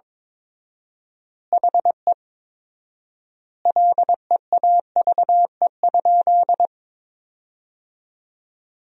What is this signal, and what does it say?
Morse code, heard as a keyed tone, about 22 wpm, 690 Hz. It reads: HE LEAVE?
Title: 4x Word-Spacing - Sentences from Top 400 Words - 22wpm